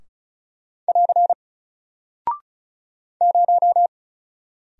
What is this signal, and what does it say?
Morse code, heard as a keyed tone, about 35 wpm, 700 Hz. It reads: <AR> 0
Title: Single Character (Letter:Number:Punctuation:Prosign) 35wpm